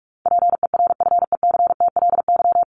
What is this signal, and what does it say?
Morse code, heard as a keyed tone, about 44 wpm, 700 Hz. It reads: PERFECTLY
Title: Top 1501-1750 Words 45wpm